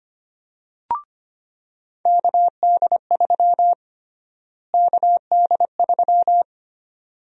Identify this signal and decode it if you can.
Morse code, heard as a keyed tone, about 25 wpm, 700 Hz. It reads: KD3 KD3
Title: USA Call Sign Prefixes - Repeated 2x - 25wpm